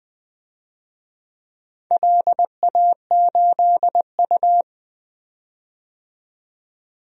Morse code, heard as a keyed tone, about 20 wpm, 700 Hz. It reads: LA8U
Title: DXCC-100 Call Signs - 4 Characters - 20wpm